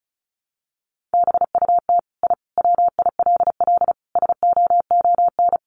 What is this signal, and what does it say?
Morse code, heard as a keyed tone, about 34 wpm, 700 Hz. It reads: BUT I WILL SOON
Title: Sentences from Top 1300 Words 35wpm